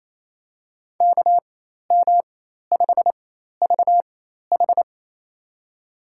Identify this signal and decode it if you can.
Morse code, heard as a keyed tone, about 27 wpm, spaced as for 13 wpm, 700 Hz. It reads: KM5VH
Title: USA Call Signs - 5 Characters - 4x Character Spacing - 28wpm